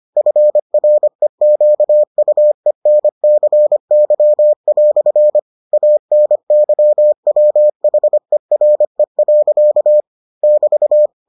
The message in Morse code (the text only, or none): FREQUENCY" ANYWHERE. <BT>